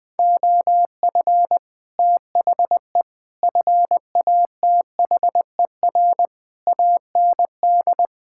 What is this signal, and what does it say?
Morse code, heard as a keyed tone, about 20 wpm, 700 Hz. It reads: OF THE FATHER AND
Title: Sets of 9 Words - 20wpm